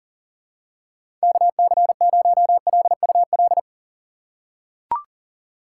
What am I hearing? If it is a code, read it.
Morse code, heard as a keyed tone, about 40 wpm, 700 Hz. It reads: KC0LUL
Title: USA Call Signs - 2x Character Spacing - 40wpm